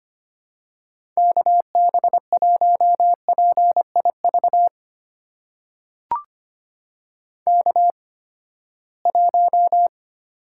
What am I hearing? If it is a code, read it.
Morse code, heard as a keyed tone, about 25 wpm, spaced as for 17 wpm, 700 Hz. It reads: KB1PIV K1
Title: USA Call Signs - 8x Character Spacing - 25wpm